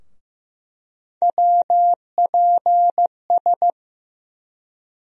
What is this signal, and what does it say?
Morse code, heard as a keyed tone, about 15 wpm, 700 Hz. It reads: WPS
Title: USA Call Sign Postfixes - 3 Letters 15wpm